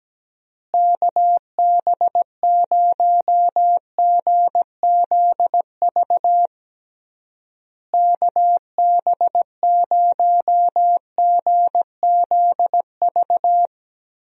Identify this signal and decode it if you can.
Morse code, heard as a keyed tone, about 17 wpm, 700 Hz. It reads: KB0GZV KB0GZV
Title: USA Call Signs - 6 Characters - Repeated 3x - 17wpm